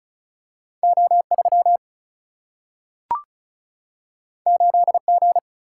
Morse code, heard as a keyed tone, about 36 wpm, 700 Hz. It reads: O 3 8 G
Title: Single-Letter Single-Number 35wpm